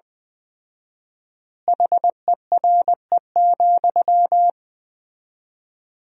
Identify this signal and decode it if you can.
Morse code, heard as a keyed tone, about 20 wpm, 700 Hz. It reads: HERE,
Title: 4x Word-Spacing - Sentences from Top 300 Words - 20wpm